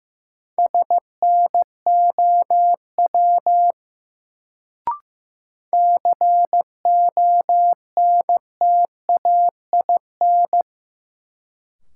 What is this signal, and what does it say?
Morse code, heard as a keyed tone, about 15 wpm, 700 Hz. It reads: SNOW CONTAIN
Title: Sentences from Top 500 Words - New Word Drill - 15wpm